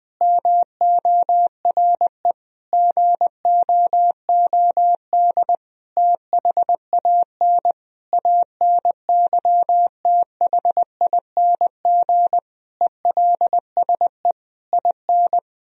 Morse code, heard as a keyed tone, about 21 wpm, 700 Hz. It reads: MORE GOOD THAN ANYTHING ELSE IN
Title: Sentences from Top 1000 Words - Review Part 4 - 20wpm